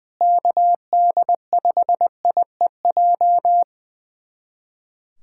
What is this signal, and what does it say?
Morse code, heard as a keyed tone, about 20 wpm, 700 Hz. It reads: KD5IEJ
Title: USA Callsign DE Callsign - 6 Characters - 20wpm